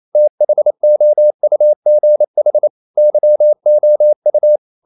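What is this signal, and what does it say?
Morse code, heard as a keyed tone, about 28 wpm, 600 Hz. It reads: THOUGH YOU
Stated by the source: Morse code practice at 28 wpm, text of Ken Follett's novel Never in English, from CW med Gnister.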